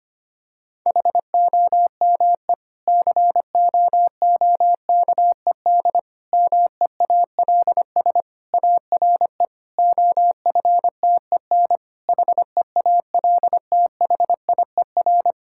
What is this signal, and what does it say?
Morse code, heard as a keyed tone, about 25 wpm, 700 Hz. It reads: HOME COOKED MEALS ARE OFTEN HEALTHIER
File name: practice / 25wpm / Difficult Sentences with Hints - 25wpm